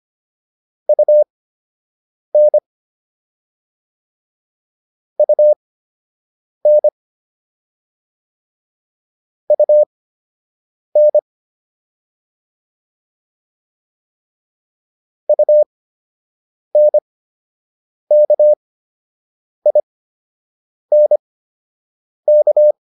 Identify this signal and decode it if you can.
Morse code, heard as a keyed tone, about 25 wpm, spaced as for 7 wpm, 600 Hz. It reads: UN UN UN UNKINK